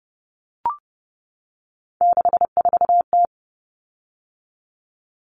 Morse code, heard as a keyed tone, about 30 wpm, 700 Hz. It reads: 64T
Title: Common Cut Numbers - 3-Digit 30wpm